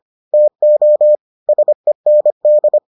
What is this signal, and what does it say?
Morse code, heard as a keyed tone, about 25 wpm, 600 Hz. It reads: TO SEND